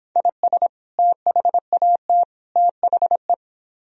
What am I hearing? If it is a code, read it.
Morse code, heard as a keyed tone, about 26 wpm, 700 Hz. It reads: IS THAT THE